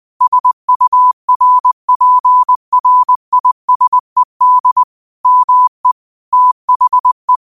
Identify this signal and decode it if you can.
Morse code, heard as a keyed tone, about 20 wpm, 1000 Hz. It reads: SURPRISED ME THE
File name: qotdfiles / 20wpm